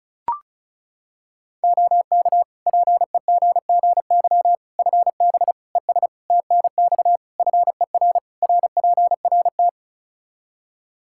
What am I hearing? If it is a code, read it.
Morse code, heard as a keyed tone, about 36 wpm, 700 Hz. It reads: OK PEGGY FB ES TNX FER RPRT
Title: Practice QSO - Exchange 2 - Info - 35wpm